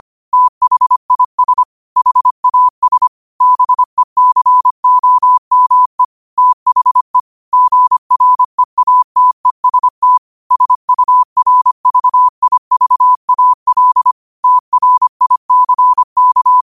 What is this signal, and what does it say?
Morse code, heard as a keyed tone, about 25 wpm, 1000 Hz. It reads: THIS HAS BECOME THE GREATEST SURVIVAL TRICK
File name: qotdfiles / 25wpm